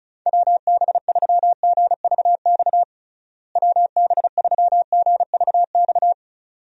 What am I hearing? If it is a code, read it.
Morse code, heard as a keyed tone, about 36 wpm, 700 Hz. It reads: WB3GVX WB3GVX
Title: USA Call Signs - 6 Characters - Repeated 2x - 35wpm